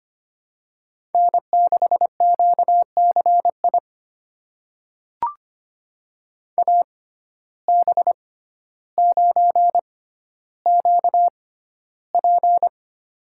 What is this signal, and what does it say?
Morse code, heard as a keyed tone, about 25 wpm, spaced as for 13 wpm, 700 Hz. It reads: N6QCI AB9QP
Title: USA Call Signs - 5 Characters - 6x Character Spacing - 25wpm